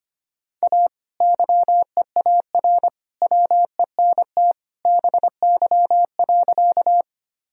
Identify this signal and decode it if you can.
Morse code, heard as a keyed tone, about 25 wpm, 700 Hz. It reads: A YEAR WENT BY.
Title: Sentences from Top 1000 Words - Review Part 1 - 25wpm